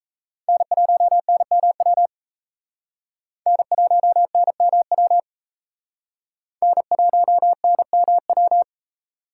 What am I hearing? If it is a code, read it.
Morse code, heard as a keyed tone, about 34 wpm, 700 Hz. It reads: N1NMW N1NMW N1NMW
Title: Speed Racing - USA 1x3 Call Signs 28wpm